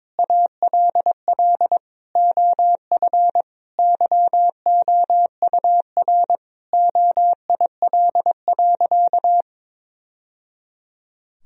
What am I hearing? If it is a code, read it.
Morse code, heard as a keyed tone, about 22 wpm, 700 Hz. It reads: ALL OF YOUR OIL.
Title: Sentences from Top 1000 Words - Review Part 4 - 22wpm